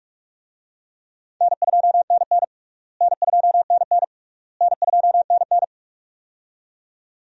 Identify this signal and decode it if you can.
Morse code, heard as a keyed tone, about 46 wpm, 700 Hz. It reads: N2NN N2NN N2NN
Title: USA Call Signs - 4 Characters - Repeated 3x - 45wpm